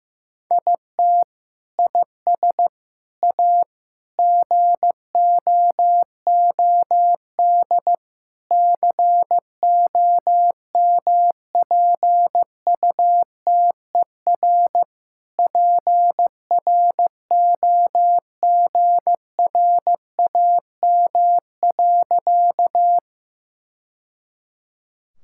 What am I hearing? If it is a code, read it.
Morse code, heard as a keyed tone, about 15 wpm, 700 Hz. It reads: IT IS A GOOD COMPUTER PROGRAM.
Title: Sentences from Top 1000 Words - Review Part 5 - 15wpm